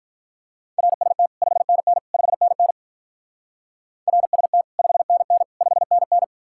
Morse code, heard as a keyed tone, about 46 wpm, 700 Hz. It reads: RST 5NN 5NN RST 5NN 5NN